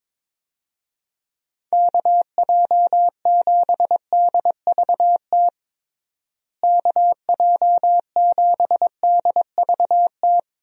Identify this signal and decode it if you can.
Morse code, heard as a keyed tone, about 22 wpm, 700 Hz. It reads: KJ7DVT KJ7DVT
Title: USA Call Signs - 6 Characters - Repeated 3x - 22wpm